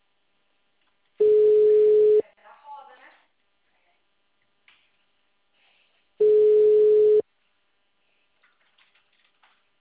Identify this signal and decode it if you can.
Ringback tone (Europe, 425 Hz), with speech.
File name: out- -2002- - - .112